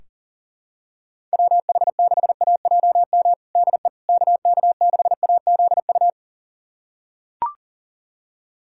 Morse code, heard as a keyed tone, about 40 wpm, 700 Hz. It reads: WH6AJM DE KK6AZU
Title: USA Callsign DE Callsign 40wpm